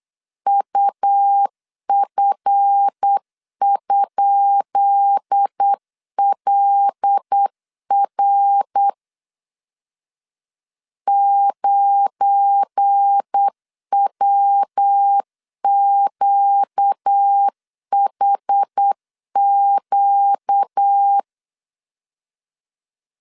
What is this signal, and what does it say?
Morse code, heard as a keyed tone, about 8 wpm, 800 Hz. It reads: UF?LR 9WQHQ
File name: LEZIONI / 10 SIGNS / 10B INT 8wpm 3534